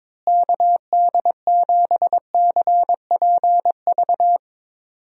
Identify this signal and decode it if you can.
Morse code, heard as a keyed tone, about 22 wpm, 700 Hz. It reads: KD7CPV
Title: USA Call Signs - 6 Characters - Repeated 2x - 22wpm